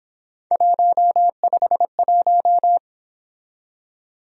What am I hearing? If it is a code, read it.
Morse code, heard as a keyed tone, about 26 wpm, 700 Hz. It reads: 151
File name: Speed-Racing - Three Letter-Number Combinations - 22wpm